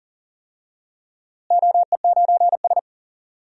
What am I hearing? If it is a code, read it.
Morse code, heard as a keyed tone, about 40 wpm, 700 Hz. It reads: OE9S
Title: DXCC-100 Call Signs - 4 Characters - Repeated 3x - 40wpm